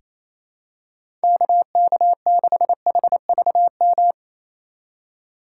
Morse code, heard as a keyed tone, about 28 wpm, 700 Hz. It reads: KK6HVM
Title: USA Call Signs - 6 Characters - 2x Character Spacing - 28wpm